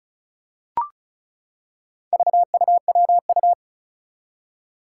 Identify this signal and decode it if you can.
Morse code, heard as a keyed tone, about 36 wpm, 700 Hz. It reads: VUWU
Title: Four Challenging Letter-Number Combinations 35wpm